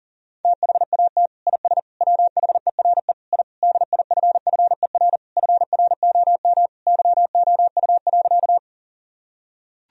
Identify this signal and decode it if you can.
Morse code, heard as a keyed tone, about 40 wpm, 700 Hz. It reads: THAT IS WHERE I DIFFER FROM YOU.
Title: Sentences from Top 1000 Words - Review Part 1 - 40wpm